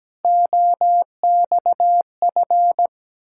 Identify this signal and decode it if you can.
Morse code, heard as a keyed tone, about 17 wpm, 700 Hz. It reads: OXF
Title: Ontario QSO Party Exchange 17wpm